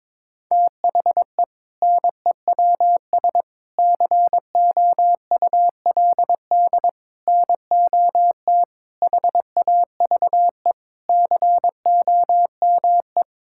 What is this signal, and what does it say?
Morse code, heard as a keyed tone, about 22 wpm, 700 Hz. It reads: THE NEWS COULD NOT HAVE COME
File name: practice / 22wpm / Sentences with 8-Word Phrases - 22wpm